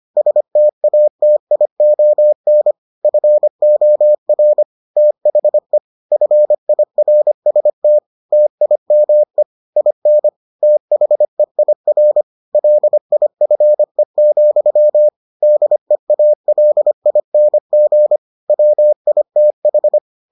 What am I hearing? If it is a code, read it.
Morse code, heard as a keyed tone, about 25 wpm, 600 Hz. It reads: STATION FOR THE FIRST TIME IN THEIR LIFE, DEALING WITH